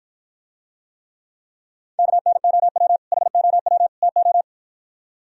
Morse code, heard as a keyed tone, about 52 wpm, 700 Hz. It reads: KNOW HOW TO